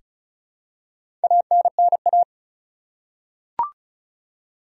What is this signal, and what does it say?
Morse code, heard as a keyed tone, about 34 wpm, 700 Hz. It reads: ANNA
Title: USA Common Names - 35wpm